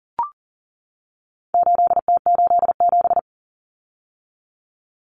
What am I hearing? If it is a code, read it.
Morse code, heard as a keyed tone, about 40 wpm, 700 Hz. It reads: 8T87
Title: Common Cut Numbers - 4-Digit 40wpm